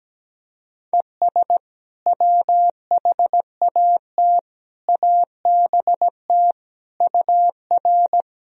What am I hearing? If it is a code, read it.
Morse code, heard as a keyed tone, about 17 wpm, 700 Hz. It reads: ES WHAT ABT UR
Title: Sentences from Top 200 Words and Abbreviations 17wpm